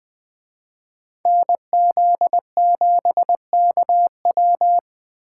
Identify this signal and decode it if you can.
Morse code, heard as a keyed tone, about 20 wpm, 700 Hz. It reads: NZ7KW
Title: Speed Racing - USA 2x2 Call Signs 17wpm